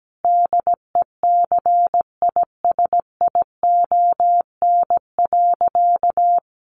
Morse code, heard as a keyed tone, about 17 wpm, 700 Hz. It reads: DECISION.
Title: Sentences from Top 1200 Words 17wpm